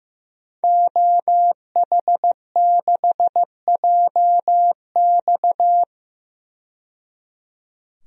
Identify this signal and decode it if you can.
Morse code, heard as a keyed tone, about 15 wpm, 700 Hz. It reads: OH6JX
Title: DXCC-100 Call Signs - 5 Characters - 15wpm